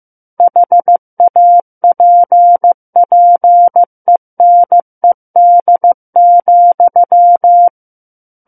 Morse code, heard as a keyed tone, about 15 wpm, 700 Hz. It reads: HAPPENED,
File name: Book One / 06-Chapter 6 - 15 WPM